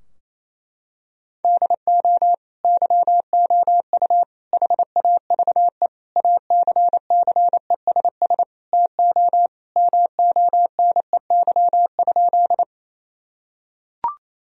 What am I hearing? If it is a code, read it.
Morse code, heard as a keyed tone, about 28 wpm, 700 Hz. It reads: DO YOU HAVE ACCESS TO MONEY?
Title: Sentences from Top 1000 Words - Review Part 4 - 28wpm